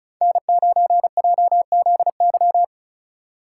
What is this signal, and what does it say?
Morse code, heard as a keyed tone, about 34 wpm, 700 Hz. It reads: N9JZY
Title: K1USN SST QSO Practice - 35wpm